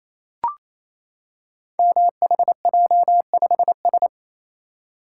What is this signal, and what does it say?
Morse code, heard as a keyed tone, about 27 wpm, 700 Hz. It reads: MHJ5S
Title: Code Groups - 5 Letter:Numbers - 28wpm